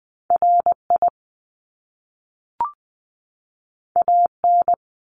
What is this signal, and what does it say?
Morse code, heard as a keyed tone, about 20 wpm, 700 Hz. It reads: RI AN
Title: Top Two Letter Combinations in Words 20wpm